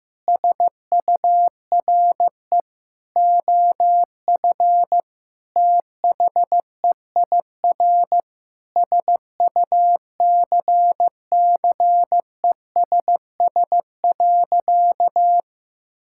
Morse code, heard as a keyed tone, about 15 wpm, 700 Hz. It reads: SURE OF THEIR SUCCESS.
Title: Sentences from Top 1000 Words - Review Part 6 - 15wpm